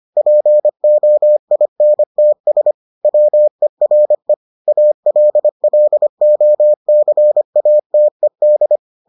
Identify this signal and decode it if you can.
Morse code, heard as a keyed tone, about 25 wpm, 600 Hz. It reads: POINTS WERE ALLOCATED